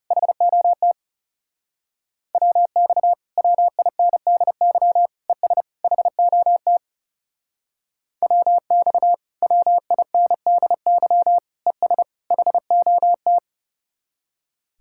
Morse code, heard as a keyed tone, about 37 wpm, 700 Hz. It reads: HOT WX WINDY ES HOT WX WINDY ES HOT
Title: Speed-Racing Common QSO Phrases - 30wpm